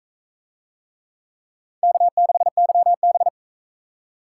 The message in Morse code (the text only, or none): K6YB